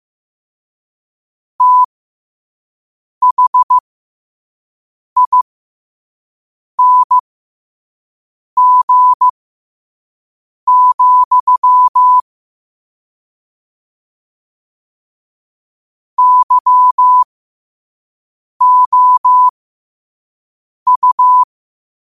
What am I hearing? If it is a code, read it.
Morse code, heard as a keyed tone, about 15 wpm, spaced as for 5 wpm, 1000 Hz. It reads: THING, YOU
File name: qotdfiles / 5wpm